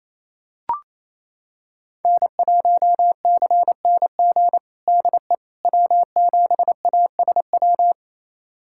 Morse code, heard as a keyed tone, about 28 wpm, 700 Hz. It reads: N1CNG DE W7ASW
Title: USA Callsign DE Callsign - 5 Characters - 28wpm